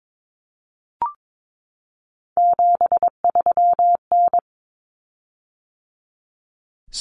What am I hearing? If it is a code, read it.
Morse code, heard as a keyed tone, about 22 wpm, 700 Hz. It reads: 73N
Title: Common Cut Numbers - 3-Digit 22wpm